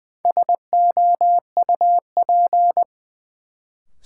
Morse code, heard as a keyed tone, about 20 wpm, 700 Hz. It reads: SOUP
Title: Sentences from Top 1000 Words - Review Part 6 - New Word Drill - 20wpm